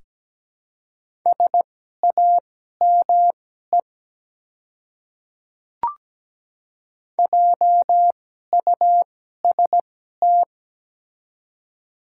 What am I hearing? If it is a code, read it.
Morse code, heard as a keyed tone, about 17 wpm, spaced as for 12 wpm, 700 Hz. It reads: SAME JUST
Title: 2x Character Spacing - Top 200 Words - 17wpm